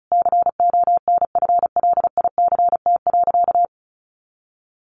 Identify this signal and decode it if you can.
Morse code, heard as a keyed tone, about 34 wpm, 700 Hz. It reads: CONFLICT.
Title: Sentences from Top 1400 Words 35wpm